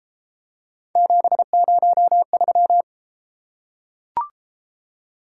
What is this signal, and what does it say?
Morse code, heard as a keyed tone, about 32 wpm, 700 Hz. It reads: 703